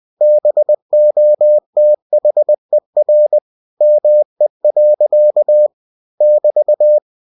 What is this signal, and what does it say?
Morse code, heard as a keyed tone, about 20 wpm, 600 Hz. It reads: BOTHER ME. <BT>
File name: cw-books / winnie-the-pooh / winnie-20wpm-page-0029